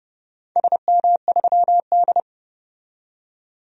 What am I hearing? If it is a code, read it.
Morse code, heard as a keyed tone, about 30 wpm, 700 Hz. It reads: SM3D
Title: DXCC-100 Call Signs - 4 Characters - 30wpm